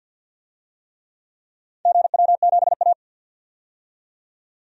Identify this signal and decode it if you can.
Morse code, heard as a keyed tone, about 53 wpm, 700 Hz. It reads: GW7A
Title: DXCC-100 Call Signs - 4 Characters - 50wpm